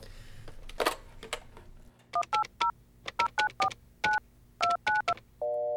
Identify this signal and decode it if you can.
DTMF, heard as telephone touch tones, tones of about 120 ms. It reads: 180 0949 292
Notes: Mit einer Spielzeugpfeife, die er in seiner Müslipackung ( "Cap'n Crunch") fand, manipulierte er in den 70er-Jahren das weltweite Telefonnetz.